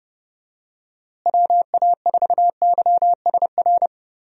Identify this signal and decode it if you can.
Morse code, heard as a keyed tone, about 30 wpm, 700 Hz. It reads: WA4YSR
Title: USA Call Signs - 6 Characters - Repeated 3x - 30wpm